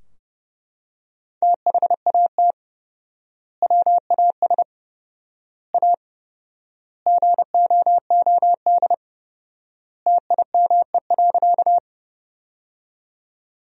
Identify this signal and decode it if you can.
Morse code, heard as a keyed tone, about 29 wpm, 700 Hz. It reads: THAT WAS A GOOD TIME.
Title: 4x Word-Spacing - Sentences from Top 200 Words - 30wpm